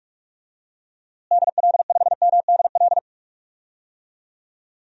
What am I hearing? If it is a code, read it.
Morse code, heard as a keyed tone, about 46 wpm, 700 Hz. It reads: DL5MDL
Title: DXCC-100 Call Signs - 6 Characters - Repeated 2x - 45wpm